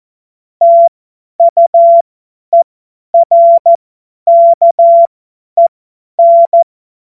Morse code, heard as a keyed tone, about 14 wpm, spaced as for 10 wpm, 680 Hz. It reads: TUERKEN
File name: Witz-30-September-2025-50-F70